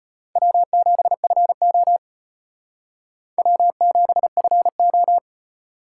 Morse code, heard as a keyed tone, about 32 wpm, 700 Hz. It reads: W7FO W7FO